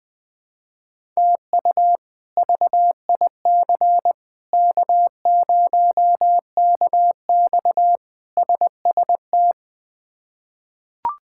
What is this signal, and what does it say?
Morse code, heard as a keyed tone, about 20 wpm, 700 Hz. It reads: TU VIC K0KX SST
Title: K1USN SST QSO Practice - 20wpm